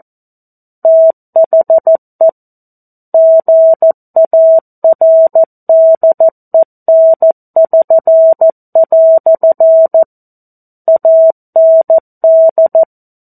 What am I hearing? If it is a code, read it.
Morse code, heard as a keyed tone, about 14 wpm, 650 Hz. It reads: THE GARDEN<SN>" AND
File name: 02-Chapter-02 14 WPM